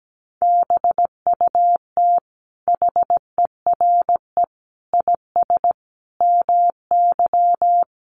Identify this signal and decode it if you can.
Morse code, heard as a keyed tone, about 17 wpm, 700 Hz. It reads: BUT HERE IS MY
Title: Sentences from Top 1500 Words 17wpm